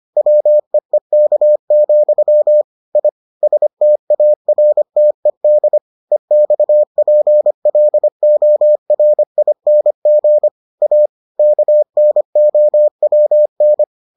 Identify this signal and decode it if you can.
Morse code, heard as a keyed tone, about 25 wpm, 600 Hz. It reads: WEEK, I STARTED EXPLORING A KNOWN